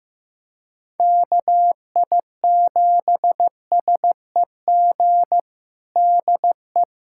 Morse code, heard as a keyed tone, about 15 wpm, 700 Hz. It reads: KI7SEG DE
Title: USA Callsign DE Callsign - 6 Characters - 15wpm